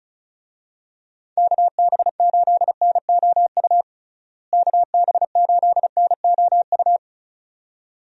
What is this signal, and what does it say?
Morse code, heard as a keyed tone, about 36 wpm, 700 Hz. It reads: KB8NOU KB8NOU
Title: USA Call Signs - 6 Characters - Repeated 2x - 35wpm